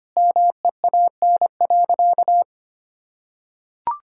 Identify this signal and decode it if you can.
Morse code, heard as a keyed tone, about 25 wpm, 700 Hz. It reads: MEAN.
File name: practice / 25wpm / Sentences with 3-Word Phrases - 25wpm